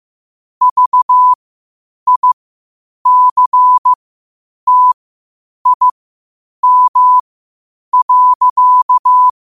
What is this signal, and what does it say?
Morse code, heard as a keyed tone, about 15 wpm, spaced as for 8 wpm, 1000 Hz. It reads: VICTIM.
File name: 8wpm